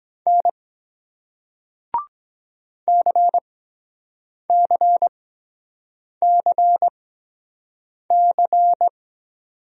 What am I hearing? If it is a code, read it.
Morse code, heard as a keyed tone, about 22 wpm, spaced as for 7 wpm, 700 Hz. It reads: N CCCC